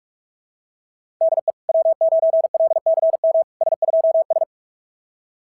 Morse code, heard as a keyed tone, about 44 wpm, 660 Hz. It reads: DE W9LCM S2S
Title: SOTA QSO Practice 45wpm